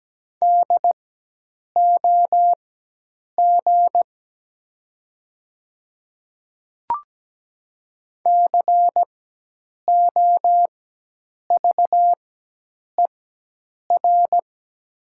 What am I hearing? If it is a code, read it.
Morse code, heard as a keyed tone, about 17 wpm, spaced as for 8 wpm, 690 Hz. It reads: DOG COVER
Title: 4x Character Spacing - Top 500 Words - 17wpm